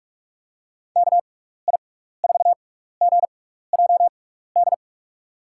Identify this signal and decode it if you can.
Morse code, heard as a keyed tone, about 46 wpm, spaced as for 16 wpm, 700 Hz. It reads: KI4GJD
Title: USA Call Signs - 6 Characters - 6x Character Spacing - 45wpm